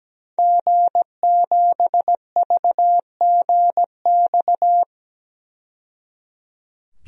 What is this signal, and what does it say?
Morse code, heard as a keyed tone, about 17 wpm, 700 Hz. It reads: G7VGX
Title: DXCC-100 Call Signs - 5 Characters - 17wpm